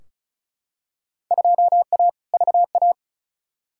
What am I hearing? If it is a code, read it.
Morse code, heard as a keyed tone, about 34 wpm, 700 Hz. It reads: 2A VA
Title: Field Day Exchanges - 35wpm